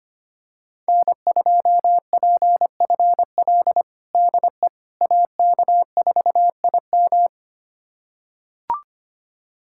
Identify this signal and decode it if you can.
Morse code, heard as a keyed tone, about 25 wpm, 700 Hz. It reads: N2PFL DE AK4IM
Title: USA Callsign DE Callsign - 5 Characters - 25wpm